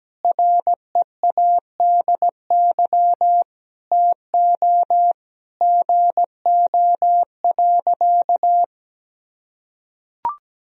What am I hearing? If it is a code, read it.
Morse code, heard as a keyed tone, about 17 wpm, 700 Hz. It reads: READY TO GO.
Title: Sentences from Top 1000 Words - Review Part 2 - 17wpm